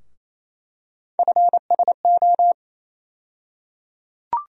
Morse code, heard as a keyed tone, about 28 wpm, 700 Hz. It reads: FSO
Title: USA Call Sign Postfixes - 3 Letters 28wpm